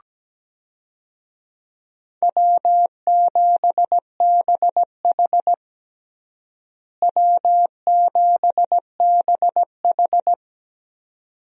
Morse code, heard as a keyed tone, about 17 wpm, 700 Hz. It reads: W7BH W7BH
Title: USA Call Signs - 4 Characters - Repeated 2x - 17wpm